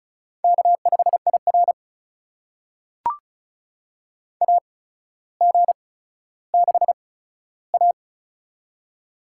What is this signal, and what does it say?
Morse code, heard as a keyed tone, about 36 wpm, spaced as for 15 wpm, 700 Hz. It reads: K5IR AG6A
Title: USA Call Signs - 8x Character Spacing - 35wpm